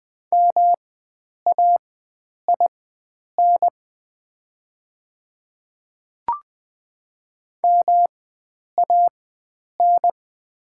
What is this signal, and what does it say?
Morse code, heard as a keyed tone, about 20 wpm, spaced as for 9 wpm, 700 Hz. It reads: MAIN MAN
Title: 4x Character Spacing - Top 300 Words - 20wpm